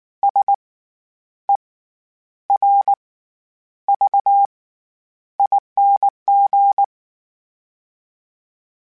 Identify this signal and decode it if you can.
Morse code, heard as a keyed tone, about 19 wpm, spaced as for 9 wpm, 800 Hz. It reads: SERVING
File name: Special INGs / Special INGs-mp3s / SlowToFaster ing @18